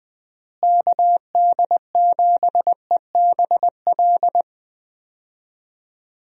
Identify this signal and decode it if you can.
Morse code, heard as a keyed tone, about 21 wpm, 700 Hz. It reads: KD7EBL
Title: USA Call Signs - 6 Characters - Repeated 3x - 20wpm